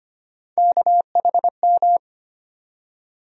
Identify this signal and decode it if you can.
Morse code, heard as a keyed tone, about 25 wpm, 700 Hz. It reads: KHM
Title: USA Call Sign Postfixes - 3 Letters 25wpm